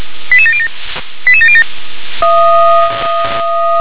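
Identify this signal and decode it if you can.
Five-tone selective call (ZVEI), heard as quick runs of five tones, tones of about 70 ms. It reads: 88707 88707
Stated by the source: ZVEI five-tone selective call, code 88694, received by SDR.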